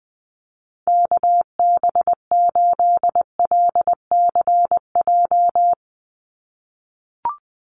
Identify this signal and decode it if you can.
Morse code, heard as a keyed tone, about 20 wpm, 700 Hz. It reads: KB8LCJ
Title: USA Call Signs 20wpm